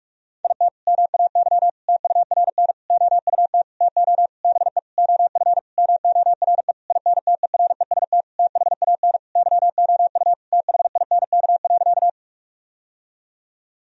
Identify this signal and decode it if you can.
Morse code, heard as a keyed tone, about 46 wpm, 700 Hz. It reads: IT MAY TURN OUT TO BE OF MORE INTEREST THAN YOU THINK.
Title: Sentences from Top 1000 Words - Review Part 2 - 45wpm